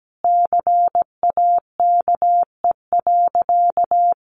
Morse code, heard as a keyed tone, about 17 wpm, 700 Hz. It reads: CAKE.
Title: Easy Phrases 17wpm